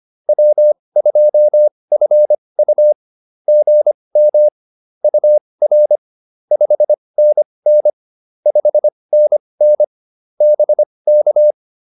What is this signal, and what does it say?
Morse code, heard as a keyed tone, about 25 wpm, spaced as for 20 wpm, 600 Hz. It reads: W2FU GM UR 5NN 5NN BK